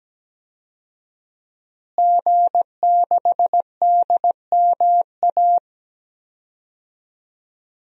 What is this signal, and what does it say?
Morse code, heard as a keyed tone, about 17 wpm, 700 Hz. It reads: G6DMA
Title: DXCC-100 Call Signs - 5 Characters - Repeated 3x - 17wpm